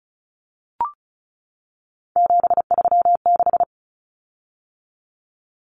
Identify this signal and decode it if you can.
Morse code, heard as a keyed tone, about 34 wpm, 700 Hz. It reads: 736
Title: Common Cut Numbers - 3-Digit 35wpm